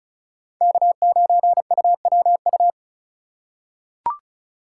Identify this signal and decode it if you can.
Morse code, heard as a keyed tone, about 36 wpm, 690 Hz. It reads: K9UWU
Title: USA Call Signs - 5 Characters - 3x Character Spacing - 35wpm